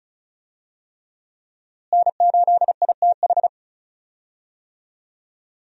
Morse code, heard as a keyed tone, about 36 wpm, 700 Hz. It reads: N8ITH
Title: USA Call Signs - 5 Characters - Repeated 3x - 35wpm